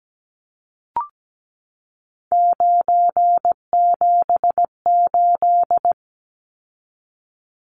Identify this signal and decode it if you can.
Morse code, heard as a keyed tone, about 17 wpm, 700 Hz. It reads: 978
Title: Numbers - 3-Digit 17wpm